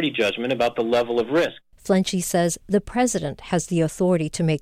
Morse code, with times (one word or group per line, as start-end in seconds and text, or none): none